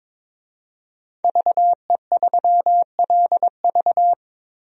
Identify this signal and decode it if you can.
Morse code, heard as a keyed tone, about 22 wpm, 700 Hz. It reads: VE3LV
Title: DXCC-100 Call Signs - 5 Characters - 22wpm